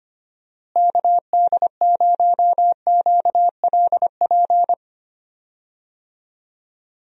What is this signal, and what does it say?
Morse code, heard as a keyed tone, about 25 wpm, 700 Hz. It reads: KD0QLP
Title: USA Call Signs - 6 Characters - Repeated 3x - 25wpm